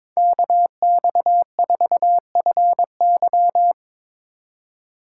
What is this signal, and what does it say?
Morse code, heard as a keyed tone, about 21 wpm, 700 Hz. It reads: KX4FY
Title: USA Call Signs - 5 Characters - 4x Character Spacing - 22wpm